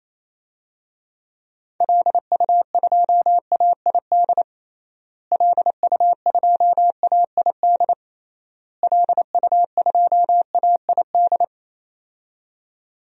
Morse code, heard as a keyed tone, about 28 wpm, 700 Hz. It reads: LU2AID LU2AID LU2AID
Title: DXCC-100 Call Signs - 6 Characters - Repeated 3x - 28wpm